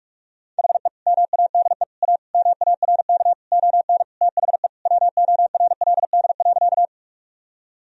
Morse code, heard as a keyed tone, about 41 wpm, 700 Hz. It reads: HE MADE A MARK ON THE WORLD.
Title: Sentences from Top 1000 Words - Review Part 1 - 45wpm